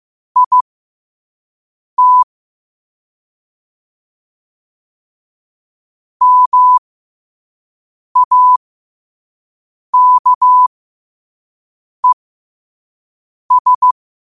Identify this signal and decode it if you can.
Morse code, heard as a keyed tone, about 15 wpm, spaced as for 5 wpm, 1000 Hz. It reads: IT MAKES
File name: 5wpm